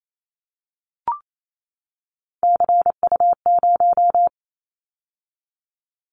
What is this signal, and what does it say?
Morse code, heard as a keyed tone, about 28 wpm, 700 Hz. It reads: CU0
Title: Callsign Prefixes for Top 50 DXCC - 28wpm